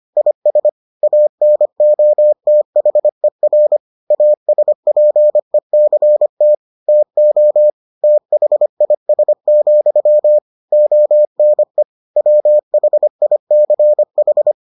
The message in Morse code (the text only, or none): IS ANOTHER ASPECT TO THIS, ONE WHICH